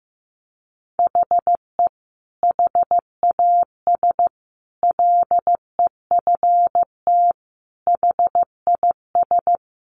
Morse code, heard as a keyed tone, about 15 wpm, 700 Hz. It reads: HE HAS LEFT HIS
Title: Sentences from Top 1100 Words 15wpm